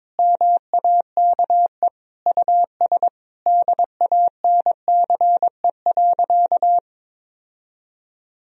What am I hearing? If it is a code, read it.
Morse code, heard as a keyed tone, about 22 wpm, 700 Hz. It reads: MAKE US DANCE.
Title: Sentences from Top 1000 Words - Review Part 4 - 22wpm